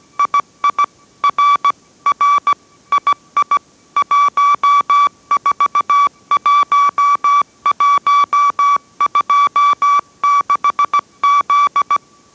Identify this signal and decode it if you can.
Morse code, heard as a keyed tone, about 17 wpm, 1150 Hz. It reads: II R R II 141126Z